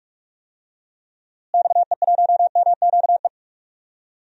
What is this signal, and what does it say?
Morse code, heard as a keyed tone, about 42 wpm, 700 Hz. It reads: XE1MQE
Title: DXCC-100 Call Signs - 6 Characters - Repeated 3x - 45wpm